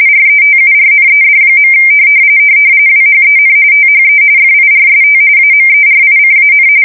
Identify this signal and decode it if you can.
RTTY, about 45.45 baud. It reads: RETURNED. WHEN THE FIRST AMERICAN
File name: fsk170 / USS-New-Jersey